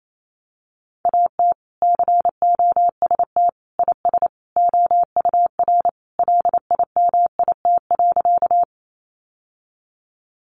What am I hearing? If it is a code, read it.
Morse code, heard as a keyed tone, about 28 wpm, 700 Hz. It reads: AT COST IS OUR LIMIT.
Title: Sentences from Top 1500 Words 28wpm